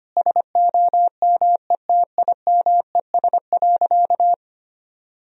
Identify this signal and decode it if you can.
Morse code, heard as a keyed tone, about 25 wpm, 700 Hz. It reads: SOMETIMES.
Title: Sentences from Top 1000 Words - Review Part 3 - 25wpm